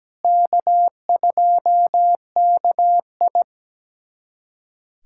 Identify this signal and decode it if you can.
Morse code, heard as a keyed tone, about 17 wpm, 700 Hz. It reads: K2KI
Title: USA Callsign DE Callsign - 4 Characters - 17wpm